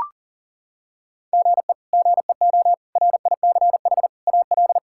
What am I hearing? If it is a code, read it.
Morse code, heard as a keyed tone, about 40 wpm, 700 Hz. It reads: GE GEO RICH AL
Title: K1USN SST QSO Practice - 40wpm